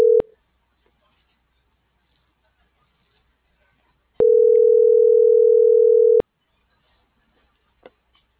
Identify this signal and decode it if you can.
Ringback tone (North America), 440 + 480 Hz.